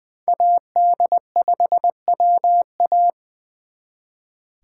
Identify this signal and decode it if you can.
Morse code, heard as a keyed tone, about 23 wpm, 700 Hz. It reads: AD5WA
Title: Speed Racing - USA 2x2 Call Signs 20wpm